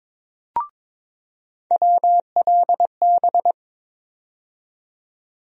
Morse code, heard as a keyed tone, about 22 wpm, 700 Hz. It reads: WLB
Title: USA Call Sign Postfixes - 3 Letters 22wpm